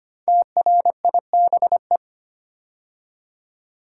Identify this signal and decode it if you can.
Morse code, heard as a keyed tone, about 25 wpm, 700 Hz. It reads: TRIBE
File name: 5 Letter Words - First 3 Letters Given - 25wpm